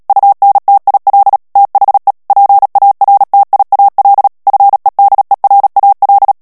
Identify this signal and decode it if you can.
Morse code, heard as a keyed tone, about 37 wpm, 800 Hz. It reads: UNTIL THE PARTIAL FEDERAL
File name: morse mp3 / ANGLAIS texte clair 3 vitesse 35 mots minute